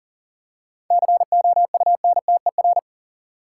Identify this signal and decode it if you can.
Morse code, heard as a keyed tone, about 40 wpm, 700 Hz. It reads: COUNTER
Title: Seven Letter Words - 40wpm